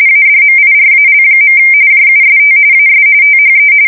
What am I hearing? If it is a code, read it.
RTTY, about 45.45 baud. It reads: ADMIRAL. JUST AS THE
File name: fsk170 / USS-New-Jersey